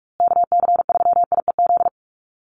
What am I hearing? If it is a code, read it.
Morse code, heard as a keyed tone, about 44 wpm, 700 Hz. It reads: KC3IEZ
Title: USA Call Signs Repeated 3x 45wpm